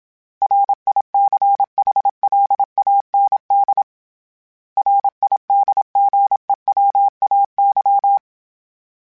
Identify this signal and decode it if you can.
Morse code, heard as a keyed tone, about 27 wpm, 800 Hz. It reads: RICHLAND RIDGEWAY
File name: fromSMs / FromSMs mp3s / Cities@25